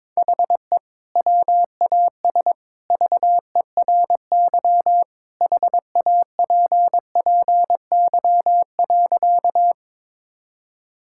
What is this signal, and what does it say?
Morse code, heard as a keyed tone, about 22 wpm, 700 Hz. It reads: HE WAS VERY HAPPY.
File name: practice / 22wpm / Sentences with 2-Word Phrases - 22wpm